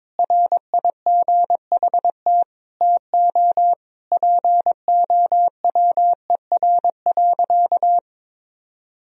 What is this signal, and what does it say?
Morse code, heard as a keyed tone, about 22 wpm, 700 Hz. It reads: RIGHT TO POWER.
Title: Sentences from Top 1000 Words - Review Part 2 - 22wpm